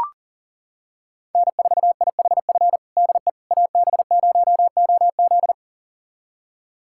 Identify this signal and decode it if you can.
Morse code, heard as a keyed tone, about 37 wpm, 700 Hz. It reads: N4IHF DE AB0OZ
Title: USA Callsign DE Callsign - 5 Characters - 40wpm